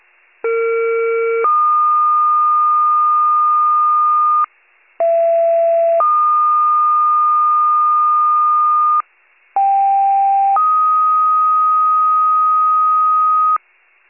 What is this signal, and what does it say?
Two-tone page, heard as a one-second tone, then a three-second tone, about 460 then 1220 Hz, about 670 then 1220 Hz, about 790 then 1250 Hz. It is heard from 0.4 to 13.6 s.